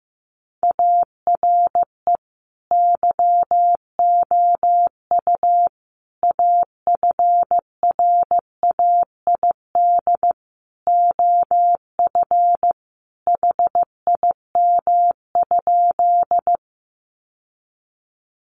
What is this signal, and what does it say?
Morse code, heard as a keyed tone, about 15 wpm, 700 Hz. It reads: ARE YOU AFRAID OF HIM?
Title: Sentences from Top 1500 Words 15wpm